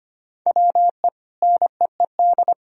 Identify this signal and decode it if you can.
Morse code, heard as a keyed tone, about 25 wpm, 700 Hz. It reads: WE NEED
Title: Sentences from Top 1000 Words - Review Part 4 - 25wpm